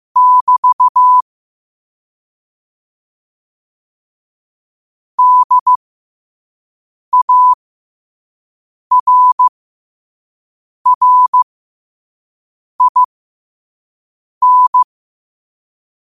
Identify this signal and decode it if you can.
Morse code, heard as a keyed tone, about 15 wpm, spaced as for 5 wpm, 1000 Hz. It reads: <BT> DARRIN
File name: qotdfiles / 5wpm